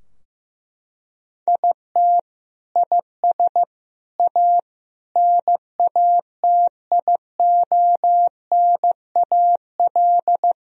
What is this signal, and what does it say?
Morse code, heard as a keyed tone, about 16 wpm, 700 Hz. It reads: IT IS A NATIONAL
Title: Sentences from Top 1000 Words - Review Part 4 - 15wpm